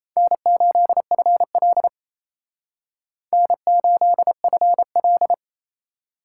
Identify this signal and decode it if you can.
Morse code, heard as a keyed tone, about 35 wpm, 700 Hz. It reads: N8FL N8FL
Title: Speed Racing - USA Call Signs 28wpm